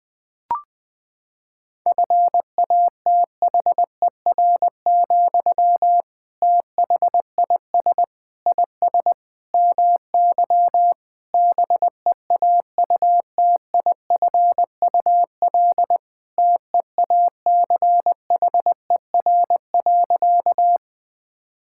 Morse code, heard as a keyed tone, about 19 wpm, 700 Hz. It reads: FATHER, THIS IS MY BEAUTIFUL TEACHER.
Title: Sentences from Top 1000 Words - Review Part 6 - 20wpm